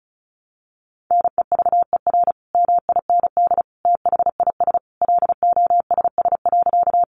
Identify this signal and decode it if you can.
Morse code, heard as a keyed tone, about 34 wpm, 700 Hz. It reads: NEVER MIND THIS LOSS.
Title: Sentences from Top 1400 Words 35wpm